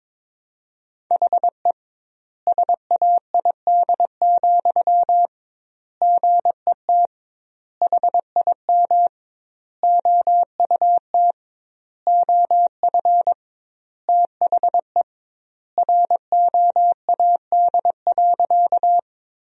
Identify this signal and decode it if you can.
Morse code, heard as a keyed tone, about 22 wpm, 700 Hz. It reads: HE SAID, GET HIM OUT OF THE ROAD.
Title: 2x Word-Spacing - Sentences from Top 400 Words - 22wpm